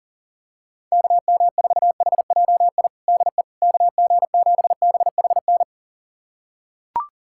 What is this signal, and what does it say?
Morse code, heard as a keyed tone, about 40 wpm, 700 Hz. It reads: KM4HJI DE KG7BHN
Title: USA Callsign DE Callsign - 6 Characters - 40wpm